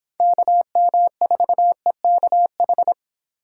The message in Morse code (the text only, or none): KM4EKH